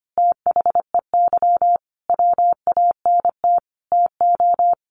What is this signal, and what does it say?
Morse code, heard as a keyed tone, about 25 wpm, 700 Hz. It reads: THEY WANT TO
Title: Easy Phrases 25wpm